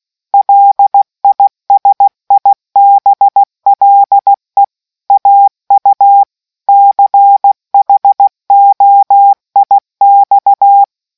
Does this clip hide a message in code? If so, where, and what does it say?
Yes: Morse code from 0.3 to 10.8 s, heard as a keyed tone, about 16 wpm, 800 Hz. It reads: LISIBLE AU CHOIX